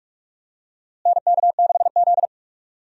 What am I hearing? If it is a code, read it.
Morse code, heard as a keyed tone, about 46 wpm, 700 Hz. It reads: NK6Z
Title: USA Call Signs - 4 Characters - Repeated 3x - 45wpm